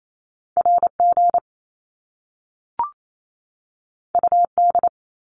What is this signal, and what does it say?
Morse code, heard as a keyed tone, about 28 wpm, 700 Hz. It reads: RG UD
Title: Top Two Letter Combinations in Words 28wpm